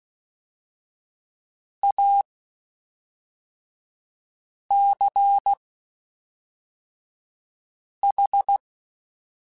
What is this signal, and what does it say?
Morse code, heard as a keyed tone, about 16 wpm, spaced as for 3 wpm, 800 Hz. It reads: ACH
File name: (spaced) / QSOs Spaced-mp3s / WD0BGP 1 @5 sp